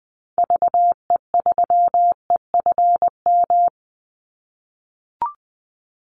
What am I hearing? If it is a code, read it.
Morse code, heard as a keyed tone, about 20 wpm, 700 Hz. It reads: VE3EFM
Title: Call Signs - 20wpm